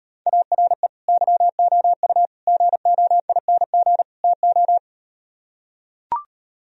Morse code, heard as a keyed tone, about 39 wpm, 700 Hz. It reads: ARE YOU GOING TO